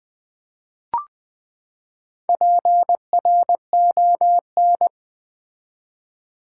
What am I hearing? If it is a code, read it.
Morse code, heard as a keyed tone, about 20 wpm, 700 Hz. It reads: PRON
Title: Build A Word - 4 Letters At A Time - 10-Letter Words - 20wpm